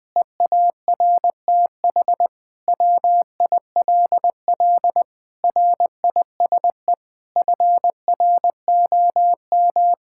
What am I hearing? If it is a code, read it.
Morse code, heard as a keyed tone, about 20 wpm, 700 Hz. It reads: EARTH WILL RISE FROM
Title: Sentences from Top 1000 Words - Review Part 6 - 20wpm